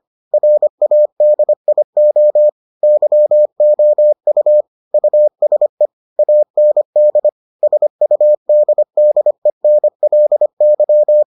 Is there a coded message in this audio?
Yes: Morse code from 0.3 to 11.2 s, heard as a keyed tone, about 25 wpm, 600 Hz. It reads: RADIO YOU USE AND SUDDENLY